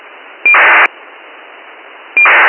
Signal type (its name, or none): ACARS aircraft data